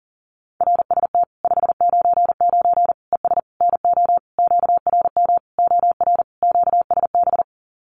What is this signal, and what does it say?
Morse code, heard as a keyed tone, about 40 wpm, 700 Hz. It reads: RST 599 ES NO QRM OR QSB
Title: Common QSO Phrases 40wpm